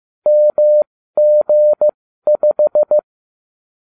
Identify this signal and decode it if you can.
Morse code, heard as a keyed tone, about 15 wpm, spaced as for 13 wpm, 600 Hz. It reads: MG5